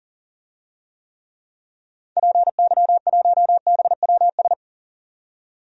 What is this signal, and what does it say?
Morse code, heard as a keyed tone, about 40 wpm, 700 Hz. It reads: PY1BWS
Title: DXCC-100 Call Signs - 6 Characters - 40wpm